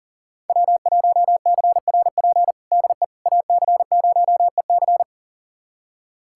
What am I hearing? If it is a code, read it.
Morse code, heard as a keyed tone, about 40 wpm, 700 Hz. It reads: W1CRP DE AC0EC
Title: USA Callsign DE Callsign - 5 Characters - 40wpm